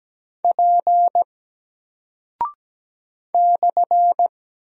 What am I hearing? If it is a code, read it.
Morse code, heard as a keyed tone, about 17 wpm, 700 Hz. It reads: P /
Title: Single Character (Letter:Number:Punctuation:Prosign) Alt 17wpm